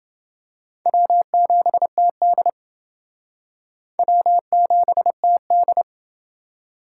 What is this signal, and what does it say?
Morse code, heard as a keyed tone, about 28 wpm, 700 Hz. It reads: W7TD W7TD